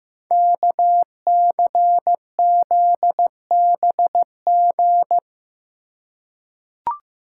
Morse code, heard as a keyed tone, about 15 wpm, 700 Hz. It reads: KCZBG
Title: Code Groups - 5 Letters - 15wpm